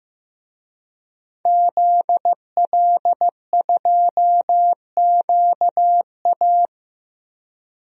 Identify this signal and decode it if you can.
Morse code, heard as a keyed tone, about 15 wpm, 700 Hz. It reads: ZL2QA
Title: DXCC-100 Call Signs - 5 Characters - Repeated 2x - 15wpm